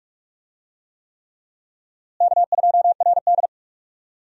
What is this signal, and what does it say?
Morse code, heard as a keyed tone, about 44 wpm, 700 Hz. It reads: K2RD
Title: USA Call Signs - 4 Characters - Repeated 3x - 45wpm